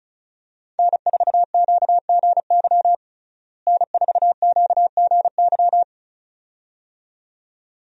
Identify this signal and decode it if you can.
Morse code, heard as a keyed tone, about 34 wpm, 700 Hz. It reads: N4QGY N4QGY
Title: USA Call Signs - 5 Characters - Repeated 2x - 35wpm